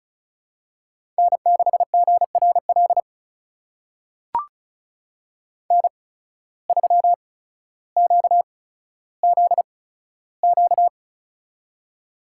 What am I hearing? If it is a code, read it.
Morse code, heard as a keyed tone, about 36 wpm, spaced as for 15 wpm, 700 Hz. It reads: N6GRL N3QZQ
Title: USA Call Signs - 5 Characters - 8x Character Spacing - 35wpm